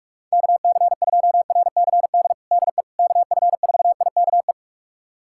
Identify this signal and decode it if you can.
Morse code, heard as a keyed tone, about 46 wpm, 700 Hz. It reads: KC2RCD DE KF4IKE
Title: USA Callsign DE Callsign 45wpm